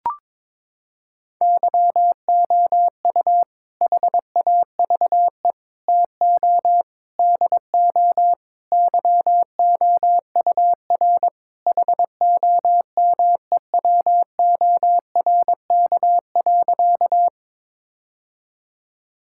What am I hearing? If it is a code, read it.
Morse code, heard as a keyed tone, about 21 wpm, 700 Hz. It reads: YOU HAVE TO DO YOUR HOMEWORK.
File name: practice / 22wpm / Sentences with 4-Word Phrases - 22wpm